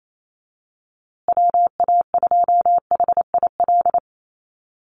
Morse code, heard as a keyed tone, about 28 wpm, 700 Hz. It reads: WA2HIL
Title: USA Call Signs Repeated 3x 28wpm